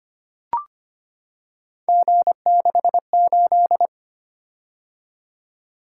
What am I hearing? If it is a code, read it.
Morse code, heard as a keyed tone, about 25 wpm, 700 Hz. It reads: G68
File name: Three Letter-Number Combinations - 25wpm